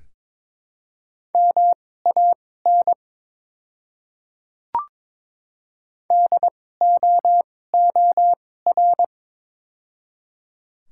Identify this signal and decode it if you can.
Morse code, heard as a keyed tone, about 22 wpm, spaced as for 16 wpm, 700 Hz. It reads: MAN DOOR
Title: 2x Character Spacing - Top 500 Words - 22wpm